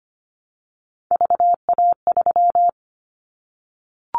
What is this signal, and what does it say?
Morse code, heard as a keyed tone, about 25 wpm, 700 Hz. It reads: VA3
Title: Callsign Prefixes for Top 25 DXCC - Repeated 3x - 25wpm